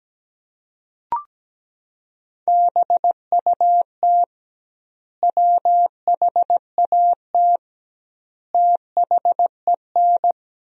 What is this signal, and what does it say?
Morse code, heard as a keyed tone, about 17 wpm, 700 Hz. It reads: BUT WHAT THEN
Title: 2x Word-Spacing - Sentences from Top 500 Words - 17wpm